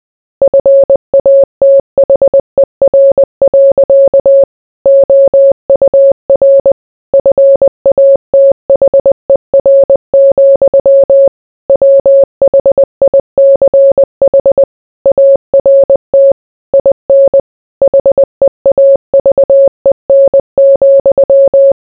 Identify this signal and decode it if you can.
Morse code, heard as a keyed tone, about 20 wpm, 550 Hz. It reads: FATHER. OUR FATHER, WHICH ART IN HEAVEN,